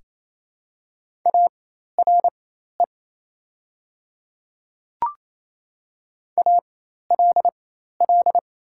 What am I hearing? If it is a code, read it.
Morse code, heard as a keyed tone, about 28 wpm, spaced as for 13 wpm, 690 Hz. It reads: ARE ALL
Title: 4x Character Spacing - Top 100 Words - 28wpm